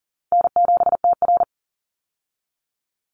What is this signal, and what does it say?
Morse code, heard as a keyed tone, about 40 wpm, 700 Hz. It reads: N7TR
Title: Call Signs Repeated 2x - 40wpm